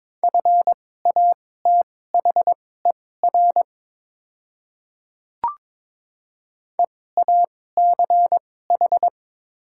Morse code, heard as a keyed tone, about 22 wpm, spaced as for 16 wpm, 700 Hz. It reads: FATHER EACH
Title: 2x Character Spacing - Top 200 Words - 22wpm